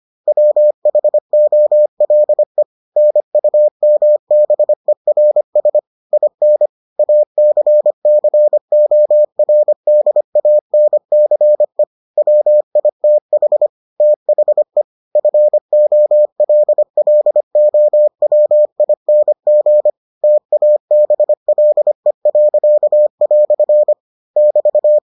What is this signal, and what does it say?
Morse code, heard as a keyed tone, about 25 wpm, 600 Hz. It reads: WHOLE NUMBERS IN ACCORDANCE WITH THE FOLLOWING TABLE." <BT>